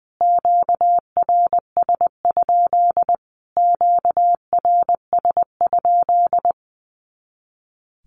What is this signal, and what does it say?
Morse code, heard as a keyed tone, about 20 wpm, 700 Hz. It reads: QRS? QRS?
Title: Q-Codes Repeated 3x 20wpm